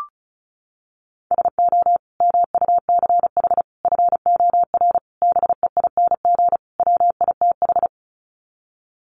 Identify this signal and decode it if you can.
Morse code, heard as a keyed tone, about 34 wpm, 700 Hz. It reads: SO MUCH FOR BEING WITH
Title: Sets of 5 Words Encore 35wpm